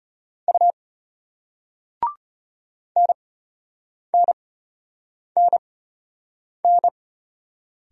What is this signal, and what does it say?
Morse code, heard as a keyed tone, about 33 wpm, spaced as for 8 wpm, 700 Hz. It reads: U NNNN